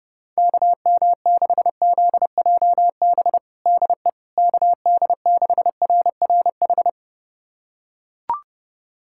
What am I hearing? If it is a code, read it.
Morse code, heard as a keyed tone, about 30 wpm, 700 Hz. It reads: KM6ZJB DE KD6RRH
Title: USA Callsign DE Callsign 30wpm